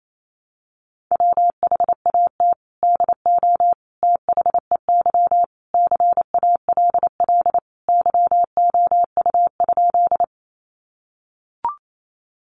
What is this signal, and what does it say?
Morse code, heard as a keyed tone, about 28 wpm, 700 Hz. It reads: WHAT DO THEY CALL YOU?
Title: Sentences from Top 100 Words 28wpm